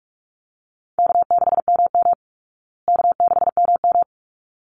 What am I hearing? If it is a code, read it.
Morse code, heard as a keyed tone, about 44 wpm, 700 Hz. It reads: K6MM K6MM
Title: Easy Call Signs Repeated 2x - 45wpm